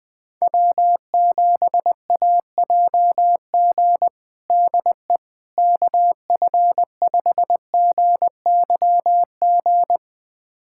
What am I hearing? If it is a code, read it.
Morse code, heard as a keyed tone, about 21 wpm, 700 Hz. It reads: W7AJG DE KF5GYG
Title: USA Callsign DE Callsign 20wpm